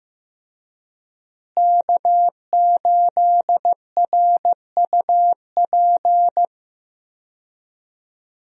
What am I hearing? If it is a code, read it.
Morse code, heard as a keyed tone, about 15 wpm, 700 Hz. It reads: K8RUP
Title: USA Call Signs - 5 Characters - Repeated 3x - 15wpm